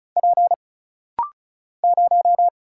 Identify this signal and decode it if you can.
Morse code, heard as a keyed tone, about 34 wpm, 700 Hz. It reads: P 0
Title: Single Letter-Number - Rapid-Fire 35wpm